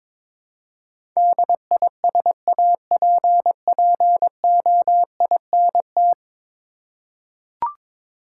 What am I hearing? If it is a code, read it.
Morse code, heard as a keyed tone, about 22 wpm, 700 Hz. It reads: DISAPPOINT
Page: morse-code-ninja-practice-22wpm